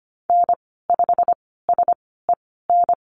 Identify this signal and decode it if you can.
Morse code, heard as a keyed tone, about 25 wpm, spaced as for 16 wpm, 700 Hz. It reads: N5SEN
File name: Lesson 15g - Number 5 - Call Signs 25wpm